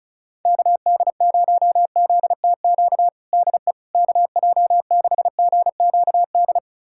Morse code, heard as a keyed tone, about 36 wpm, 700 Hz. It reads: KD0ZTQ DE KJ6GQD
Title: USA Callsign DE Callsign 35wpm